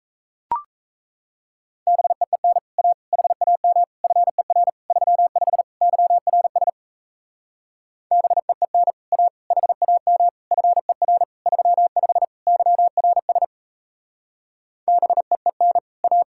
Morse code, heard as a keyed tone, about 37 wpm, 700 Hz. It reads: BEEN A HAM FER 35 YRS BEEN A HAM FER 35 YRS BEEN A